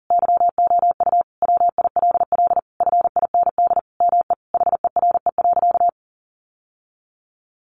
Morse code, heard as a keyed tone, about 37 wpm, 700 Hz. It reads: YOU WILL FIND ME HERE.
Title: Sentences from Top 1200 Words 40wpm